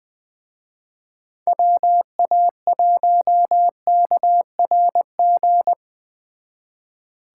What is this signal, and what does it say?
Morse code, heard as a keyed tone, about 20 wpm, 700 Hz. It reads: WA1KRG
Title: Speed Racing - USA 2x3 Call Signs 17wpm